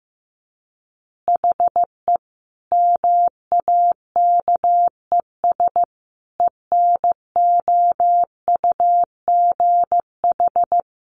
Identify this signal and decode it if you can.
Morse code, heard as a keyed tone, about 15 wpm, 700 Hz. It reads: HE MAKES ENOUGH
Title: Additional Easy Phrases 15wpm